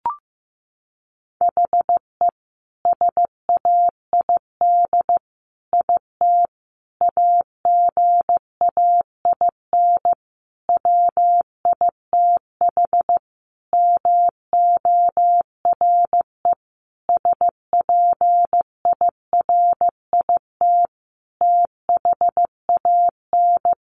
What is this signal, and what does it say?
Morse code, heard as a keyed tone, about 15 wpm, 700 Hz. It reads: HE SAID IT AGAIN WITH MORE SPIRIT THAN
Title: Sentences from Top 1400 Words 15wpm